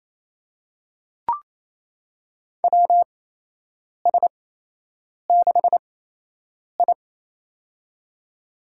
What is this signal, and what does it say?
Morse code, heard as a keyed tone, about 28 wpm, spaced as for 8 wpm, 700 Hz. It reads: WS6I
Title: USA Call Signs - 4 Characters - 8x Character Spacing - 28wpm